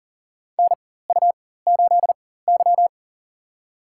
Morse code, heard as a keyed tone, about 40 wpm, spaced as for 19 wpm, 700 Hz. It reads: NU8Y
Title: USA Call Signs - 4x Character Spacing - 40wpm